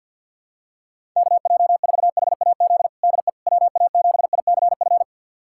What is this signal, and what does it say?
Morse code, heard as a keyed tone, about 48 wpm, 700 Hz. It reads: KJ4HAZ DE WA7ICF
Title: USA Callsign DE Callsign - 6 Characters - 50wpm